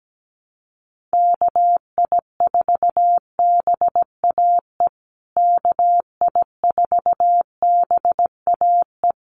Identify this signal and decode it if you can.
Morse code, heard as a keyed tone, about 17 wpm, 700 Hz. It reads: KI4BAE KI4BAE
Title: USA Call Signs Repeated 2x 17wpm